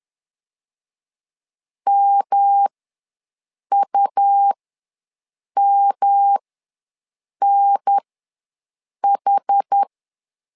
Morse code, heard as a keyed tone, about 10 wpm, spaced as for 6 wpm, 800 Hz. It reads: MUMNH
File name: LEZIONI / 3 MWRO / 3 M ws 10wpm 2121